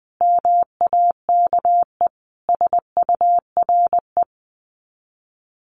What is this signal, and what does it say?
Morse code, heard as a keyed tone, about 20 wpm, 700 Hz. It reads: MAKE SURE
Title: Sets of 5 Words Encore2 20wpm